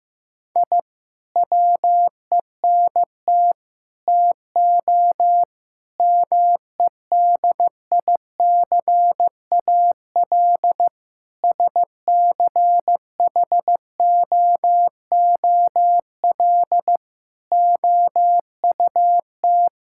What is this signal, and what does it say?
Morse code, heard as a keyed tone, about 15 wpm, 700 Hz. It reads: I WENT TO MEDICAL SCHOOL OUT
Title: Sentences from Top 1000 Words - Review Part 5 - 15wpm